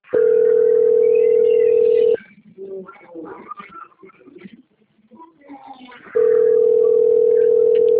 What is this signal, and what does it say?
Ringback tone (North America), 440 + 480 Hz.